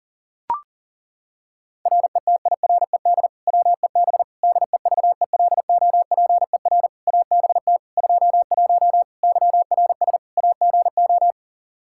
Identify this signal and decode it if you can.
Morse code, heard as a keyed tone, about 37 wpm, 700 Hz. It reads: RETIRED WEB DEVELOPER ABT 21 YRS AGO
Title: Practice QSO - Exchange 3 - More Info - 40wpm